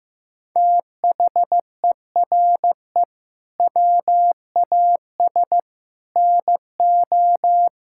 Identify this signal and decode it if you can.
Morse code, heard as a keyed tone, about 15 wpm, 700 Hz. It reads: THERE WAS NO
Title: Sentences from Top 1600 Words 15wpm